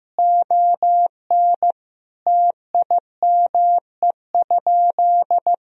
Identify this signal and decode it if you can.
Morse code, heard as a keyed tone, about 15 wpm, 700 Hz. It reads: ON TIME?
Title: Sentences from Top 1000 Words - Review Part 3 - 15wpm